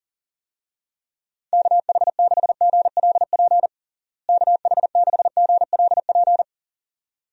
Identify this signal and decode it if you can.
Morse code, heard as a keyed tone, about 40 wpm, 700 Hz. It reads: KH6GLP KH6GLP
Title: DXCC-100 Call Signs - 6 Characters - Repeated 2x - 40wpm